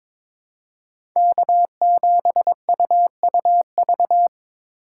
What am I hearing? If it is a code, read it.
Morse code, heard as a keyed tone, about 22 wpm, 700 Hz. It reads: K7UUV
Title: USA Call Signs - 5 Characters - 2x Character Spacing - 22wpm